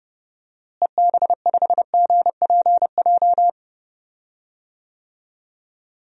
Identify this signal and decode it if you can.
Morse code, heard as a keyed tone, about 32 wpm, 700 Hz. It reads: EB5GPJ
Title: DXCC-100 Call Signs - 6 Characters - Repeated 2x - 30wpm